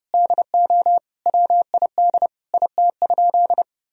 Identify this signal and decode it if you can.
Morse code, heard as a keyed tone, about 32 wpm, 700 Hz. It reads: DO WID IT?
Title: Sentences from Top 100 Words and Abbreviations 30wpm